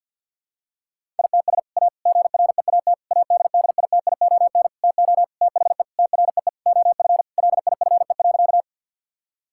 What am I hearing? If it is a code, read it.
Morse code, heard as a keyed tone, about 50 wpm, 700 Hz. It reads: ITS A GREAT ADDITION TO THE TREE OF LIFE.
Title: Sentences from Top 1000 Words - Review Part 6 - 50wpm